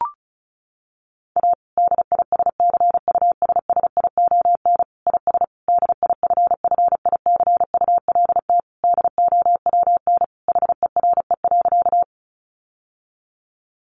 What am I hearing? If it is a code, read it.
Morse code, heard as a keyed tone, about 34 wpm, 700 Hz. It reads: A DISCUSSION IS DIFFICULT DOWN HERE.
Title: Sentences from Top 1100 Words 35wpm